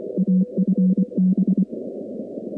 Morse code, heard as a keyed tone, about 24 wpm, 190 Hz. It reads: AFB
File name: contests / 2019-cw-wpx-kt5j-k5pi- -2330 0-19